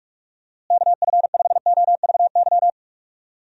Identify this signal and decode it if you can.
Morse code, heard as a keyed tone, about 41 wpm, 700 Hz. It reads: KF5OVY
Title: USA Call Signs - 6 Characters - Repeated 2x - 45wpm